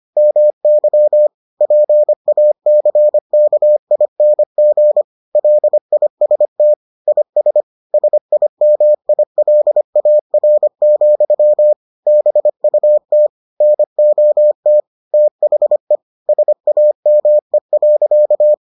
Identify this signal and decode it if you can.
Morse code, heard as a keyed tone, about 25 wpm, 600 Hz. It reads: MY PACKING LIST IS SIMILAR, BUT NOT THE SAME.